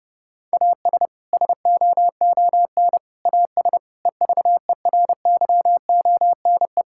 Morse code, heard as a keyed tone, about 32 wpm, 700 Hz. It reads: AS SOON AS EVERYONE
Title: Sentences from Top 1000 Words - Review Part 5 - 30wpm